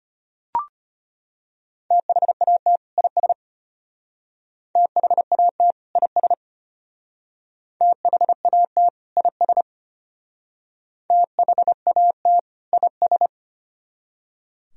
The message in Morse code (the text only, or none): THAT IS THAT IS THAT IS THAT IS